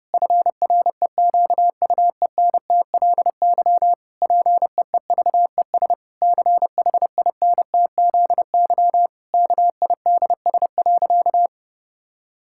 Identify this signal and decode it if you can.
Morse code, heard as a keyed tone, about 30 wpm, 700 Hz. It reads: FREQUENTLY PEEVES CHINTZY KIDS.
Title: Pangram Challenge v2 - 30wpm